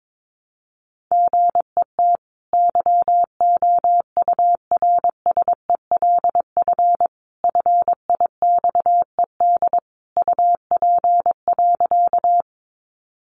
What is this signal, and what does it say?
Morse code, heard as a keyed tone, about 22 wpm, 700 Hz. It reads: GET YOURSELF FIXED UP.
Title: Sentences from Top 1100 Words 22wpm